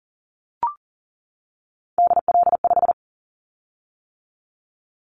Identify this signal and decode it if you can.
Morse code, heard as a keyed tone, about 41 wpm, 700 Hz. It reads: DL5
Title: Callsign Prefixes for Top 50 DXCC - 40wpm